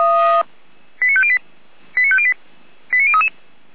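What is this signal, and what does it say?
Five-tone selective call (ZVEI), heard as quick runs of five tones, tones of about 70 ms. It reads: 89508 89508 89033
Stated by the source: ZVEI five-tone selective call 89058 received on SDR, served by feuerwehr-orlamuende (f85360).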